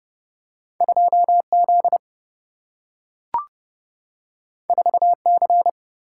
Morse code, heard as a keyed tone, about 30 wpm, 700 Hz. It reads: 2 Z 4 C
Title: Single-Letter Single-Number 30wpm